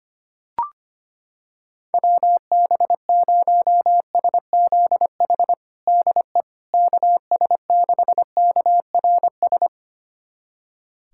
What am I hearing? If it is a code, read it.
Morse code, heard as a keyed tone, about 25 wpm, 700 Hz. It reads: WB0SZH DE KS6KRS
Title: USA Callsign DE Callsign 25wpm